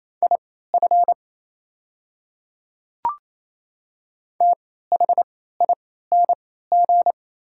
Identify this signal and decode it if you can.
Morse code, heard as a keyed tone, about 28 wpm, spaced as for 16 wpm, 700 Hz. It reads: IF THING
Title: 3x Character Spacing - Top 200 Words - 28wpm